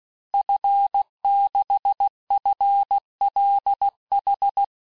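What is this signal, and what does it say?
Morse code, heard as a keyed tone, about 16 wpm, 800 Hz. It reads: F6FLH
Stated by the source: Morse course text, end of QSO, sent at 15 words per minute.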